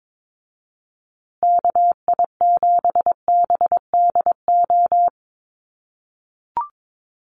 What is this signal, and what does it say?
Morse code, heard as a keyed tone, about 22 wpm, 700 Hz. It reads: KI7BDO
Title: USA Call Signs Repeated 2x 22wpm